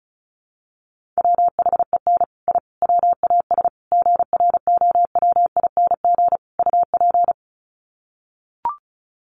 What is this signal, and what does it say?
Morse code, heard as a keyed tone, about 34 wpm, 700 Hz. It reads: WHEN I WAS GROWING UP
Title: Sets of 5 Words Encore 35wpm